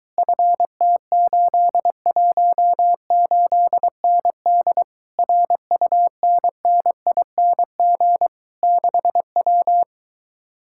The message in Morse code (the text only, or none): FT818ND RUNNING 6W